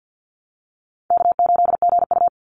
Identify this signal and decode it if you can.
Morse code, heard as a keyed tone, about 50 wpm, 700 Hz. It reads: K8GU
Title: Easy Call Signs Repeated 2x - 50wpm